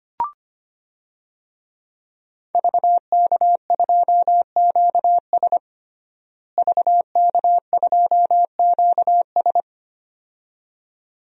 Morse code, heard as a keyed tone, about 25 wpm, 700 Hz. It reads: VK2QS VK2QS
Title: DXCC-100 Call Signs - 5 Characters - Repeated 2x - 25wpm